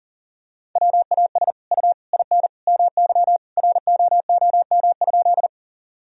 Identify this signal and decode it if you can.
Morse code, heard as a keyed tone, about 40 wpm, 700 Hz. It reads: WAS U IN MY ROOM?
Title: Sentences from Top 300 Words and Abbreviations 40wpm